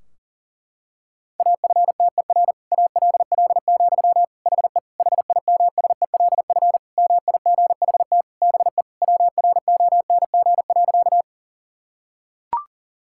Morse code, heard as a keyed tone, about 40 wpm, 700 Hz. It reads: AFTER ALL, HE HIMSELF MIGHT BE WRONG.
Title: Sentences from Top 1600 Words 40wpm